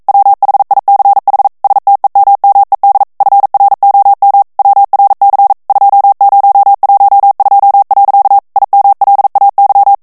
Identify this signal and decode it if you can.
Morse code, heard as a keyed tone, about 42 wpm, 800 Hz. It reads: WHICH STEMMED FROM WRC 2012. IMLAY